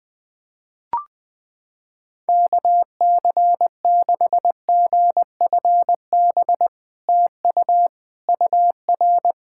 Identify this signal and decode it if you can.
Morse code, heard as a keyed tone, about 20 wpm, 700 Hz. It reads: KC6GFB TU UR
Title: SOTA QSO Practice 20wpm